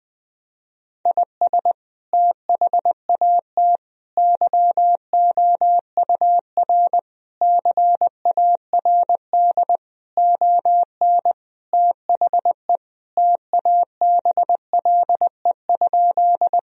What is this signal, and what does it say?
Morse code, heard as a keyed tone, about 20 wpm, 700 Hz. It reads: IS THAT YOUR CARD ON THE TABLE?
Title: Sentences from Top 1000 Words - Review Part 4 - 20wpm